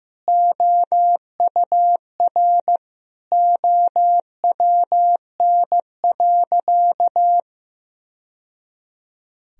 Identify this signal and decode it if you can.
Morse code, heard as a keyed tone, about 15 wpm, 700 Hz. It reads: OUR OWN.
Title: Sentences from Top 1000 Words - Review Part 1 - 15wpm